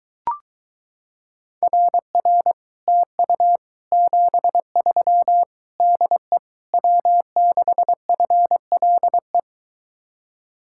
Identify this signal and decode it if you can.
Morse code, heard as a keyed tone, about 23 wpm, 700 Hz. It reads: RR TU 73 DE W6FLE